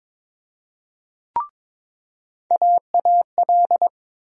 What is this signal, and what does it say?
Morse code, heard as a keyed tone, about 22 wpm, 700 Hz. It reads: AAL
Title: USA Call Sign Postfixes - 3 Letters 22wpm